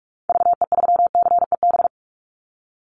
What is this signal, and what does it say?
Morse code, heard as a keyed tone, about 44 wpm, 700 Hz. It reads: VE3CEB
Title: Call Signs - 45wpm